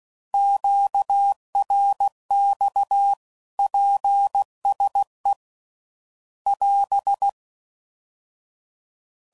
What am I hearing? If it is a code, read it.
Morse code, heard as a keyed tone, about 16 wpm, 800 Hz. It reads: QRX PSE <AS>
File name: CW Code Groups @15WPM / QSOs / WA7QOX QRX